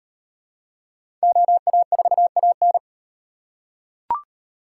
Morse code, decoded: OA4AN

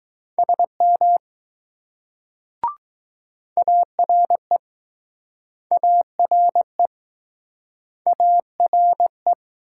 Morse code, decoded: SM ARE ARE ARE

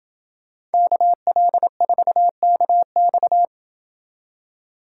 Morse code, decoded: KL4KX

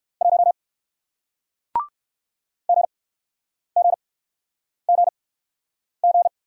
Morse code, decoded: 1 GGGG

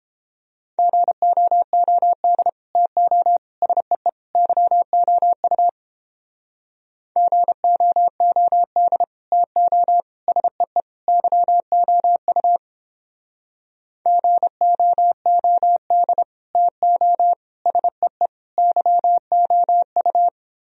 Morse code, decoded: GOOD TO SEE YOU GOOD TO SEE YOU GOOD TO SEE YOU

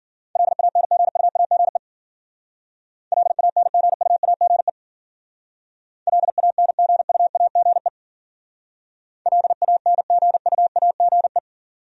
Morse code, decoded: LANGUAGE LANGUAGE LANGUAGE LANGUAGE

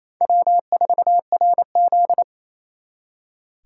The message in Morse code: W4RZ

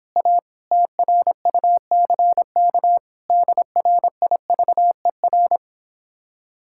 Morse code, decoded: A TRUCK DRIVER

